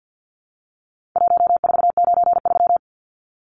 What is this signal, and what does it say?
Morse code, heard as a keyed tone, about 50 wpm, 700 Hz. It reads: 1493
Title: Numbers - Contest Style - 4-Digit 50wpm